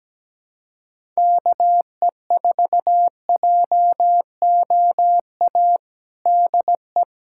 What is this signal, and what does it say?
Morse code, heard as a keyed tone, about 17 wpm, 700 Hz. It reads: KE4JOA DE
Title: USA Callsign DE Callsign - 6 Characters - 17wpm